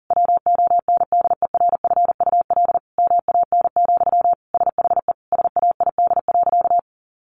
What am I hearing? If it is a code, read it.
Morse code, heard as a keyed tone, about 40 wpm, 700 Hz. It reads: WONDERFUL MAN, SHE SAID.
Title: Sentences from Top 1100 Words 40wpm